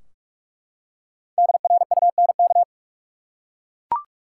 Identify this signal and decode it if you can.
Morse code, heard as a keyed tone, about 44 wpm, 700 Hz. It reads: DRUNK
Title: Five Letter Words - 45wpm